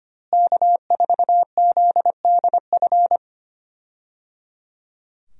Morse code, heard as a keyed tone, about 25 wpm, 700 Hz. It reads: K4ZDF
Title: USA Call Signs - 5 Characters - Repeated 3x - 25wpm